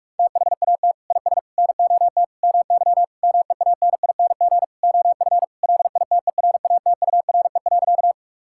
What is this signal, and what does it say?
Morse code, heard as a keyed tone, about 46 wpm, 700 Hz. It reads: THAT IS NOT MY MEANING OF LITERATURE.
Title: Sentences from Top 1000 Words - Review Part 6 - 45wpm